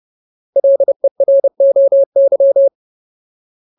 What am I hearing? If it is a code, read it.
Morse code, heard as a keyed tone, about 30 wpm, 550 Hz. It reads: LEROY
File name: Ultimate Word List / 30WPM Ultimate